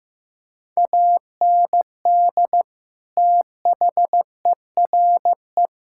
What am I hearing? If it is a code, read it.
Morse code, heard as a keyed tone, about 15 wpm, 700 Hz. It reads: AND THERE
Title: Sentences from Top 1600 Words 15wpm